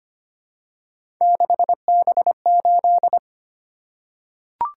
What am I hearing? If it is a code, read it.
Morse code, heard as a keyed tone, about 25 wpm, 700 Hz. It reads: 6B8
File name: Three Letter-Number Combinations - 25wpm